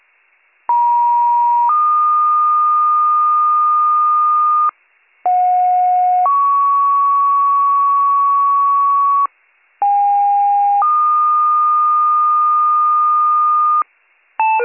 Two-tone page, heard as a one-second tone, then a three-second tone, about 950 then 1220 Hz, about 730 then 1120 Hz, about 810 then 1220 Hz. It is heard from 0.7 to 13.8 s.